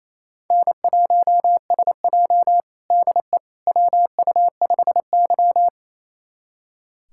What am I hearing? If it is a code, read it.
Morse code, heard as a keyed tone, about 28 wpm, 700 Hz. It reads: N1SJ DE WU5Y
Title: USA Callsign DE Callsign - 4 Characters - 28wpm